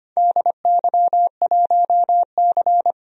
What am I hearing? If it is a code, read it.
Morse code, heard as a keyed tone, about 25 wpm, 700 Hz. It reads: DY1C
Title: DXCC-100 Call Signs - 4 Characters - 25wpm